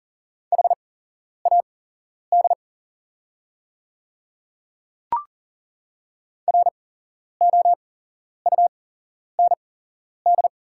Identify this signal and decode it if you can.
Morse code, heard as a keyed tone, about 40 wpm, spaced as for 11 wpm, 690 Hz. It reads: HAD ROUND
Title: 8x Character Spacing - Top 200 Words - 40wpm